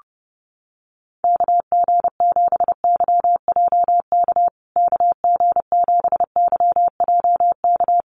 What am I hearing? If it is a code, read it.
Morse code, heard as a keyed tone, about 30 wpm, 700 Hz. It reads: KG7YJK KG7YJK
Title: USA Call Signs Repeated 3x 30wpm